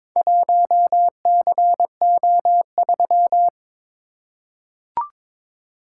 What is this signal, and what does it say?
Morse code, heard as a keyed tone, about 22 wpm, 700 Hz. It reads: 1CO3
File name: Four Letter-Number Combinations - 22wpm